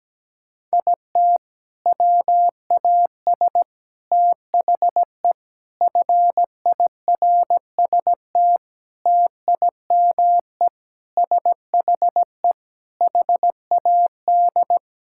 Morse code, heard as a keyed tone, about 17 wpm, 700 Hz. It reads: IT WAS THE FIRST TIME SHE HAD
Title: Sentences from Top 1000 Words - Review Part 6 - 17wpm